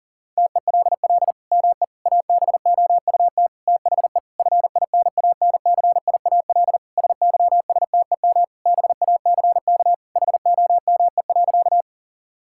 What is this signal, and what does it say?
Morse code, heard as a keyed tone, about 40 wpm, 700 Hz. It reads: TELL ME ABOUT THE FINANCIAL SYSTEM BACK HOME.
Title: Sentences from Top 1000 Words - Review Part 5 - 40wpm